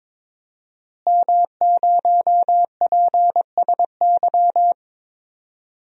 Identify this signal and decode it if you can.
Morse code, heard as a keyed tone, about 22 wpm, 700 Hz. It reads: M0PSY
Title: DXCC-100 Call Signs - 5 Characters - Repeated 2x - 22wpm